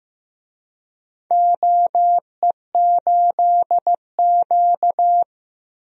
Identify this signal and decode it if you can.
Morse code, heard as a keyed tone, about 15 wpm, 700 Hz. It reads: OE8Q
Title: DXCC-100 Call Signs - 4 Characters - 15wpm